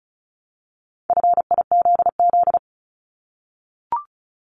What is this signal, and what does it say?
Morse code, heard as a keyed tone, about 36 wpm, 700 Hz. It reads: FIZZ
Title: Multi-letter Phonemes 35wpm